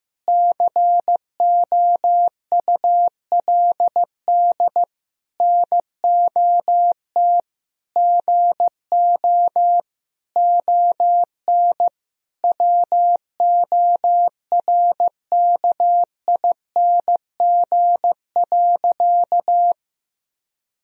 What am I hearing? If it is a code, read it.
Morse code, heard as a keyed tone, about 15 wpm, 700 Hz. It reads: COULD NOT GO ON WORKING.
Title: Sentences from Top 1000 Words - Review Part 4 - 15wpm